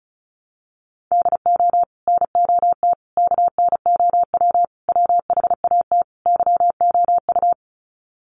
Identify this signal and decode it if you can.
Morse code, heard as a keyed tone, about 34 wpm, 700 Hz. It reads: DO NOT KNOW WHAT YOU
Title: Sets of 5 Words 35wpm